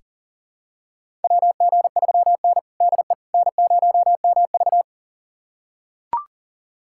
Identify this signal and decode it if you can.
Morse code, heard as a keyed tone, about 40 wpm, 700 Hz. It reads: WG3N DE N0MV
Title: USA Callsign DE Callsign - 4 Characters - 40wpm